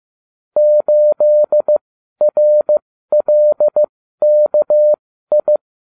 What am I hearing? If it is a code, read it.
Morse code, heard as a keyed tone, about 15 wpm, spaced as for 12 wpm, 600 Hz. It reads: 8RLKI